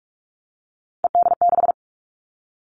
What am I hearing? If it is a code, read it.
Morse code, heard as a keyed tone, about 44 wpm, 700 Hz. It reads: ED6
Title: Callsign Prefixes for Top 50 DXCC - 45wpm